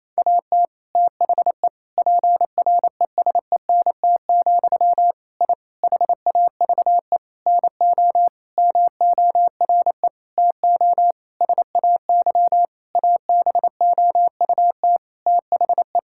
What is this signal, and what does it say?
Morse code, heard as a keyed tone, about 28 wpm, 700 Hz. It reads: AT THE PRESENT, I HAVE NO MORE TO SAY ABOUT THE
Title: Sentences from Top 1000 Words - Review Part 5 - 28wpm